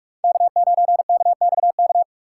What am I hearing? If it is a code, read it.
Morse code, heard as a keyed tone, about 43 wpm, 700 Hz. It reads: K9KXK